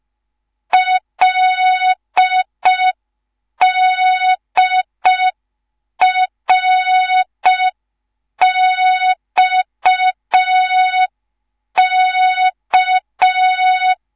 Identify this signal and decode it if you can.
Morse code, heard as a keyed tone, about 5 wpm, 3700 Hz. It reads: LDRXK